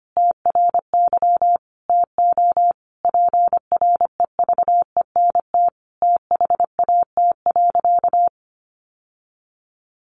Morse code, heard as a keyed tone, about 25 wpm, 700 Hz. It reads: TRY TO PREVENT THAT.
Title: Sentences from Top 1100 Words 25wpm